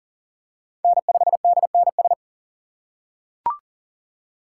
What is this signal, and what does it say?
Morse code, heard as a keyed tone, about 37 wpm, 700 Hz. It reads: N5DNS
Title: USA Call Signs - 5 Characters - 4x Character Spacing - 40wpm